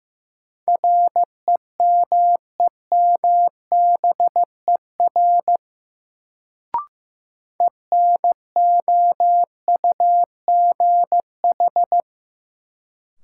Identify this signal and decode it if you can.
Morse code, heard as a keyed tone, about 15 wpm, 700 Hz. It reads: REMEMBER ENOUGH
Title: Sentences from Top 400 Words - New Word Drill - 15wpm